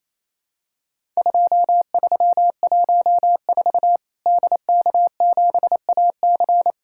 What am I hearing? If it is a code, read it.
Morse code, heard as a keyed tone, about 28 wpm, 700 Hz. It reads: 2314 DK7AC
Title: Worked All Europe DX Contest - QTC Exchange - 28wpm